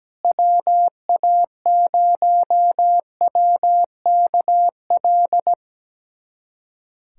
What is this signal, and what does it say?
Morse code, heard as a keyed tone, about 17 wpm, 700 Hz. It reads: WA0WKL
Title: USA Callsign DE Callsign 17wpm